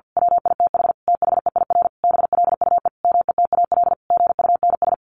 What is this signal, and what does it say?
Morse code, heard as a keyed tone, about 50 wpm, 700 Hz. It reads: WITH THEIR BLUE METAL GUNS
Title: A Sound Of Thunder 50wpm